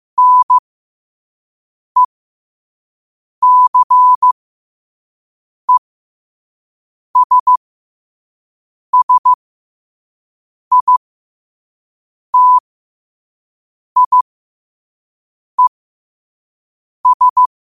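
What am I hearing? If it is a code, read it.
Morse code, heard as a keyed tone, about 15 wpm, spaced as for 5 wpm, 1000 Hz. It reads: NECESSITIES
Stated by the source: Morse code practice at 5 wpm.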